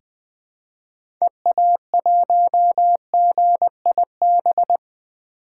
Morse code, heard as a keyed tone, about 20 wpm, 700 Hz. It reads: EA1GIB
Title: 2023 POTA Hunter Call Signs - 20wpm